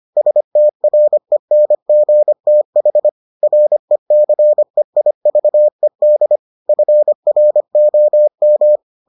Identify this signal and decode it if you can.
Morse code, heard as a keyed tone, about 25 wpm, 600 Hz. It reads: STRENGTH RECEIVED FROM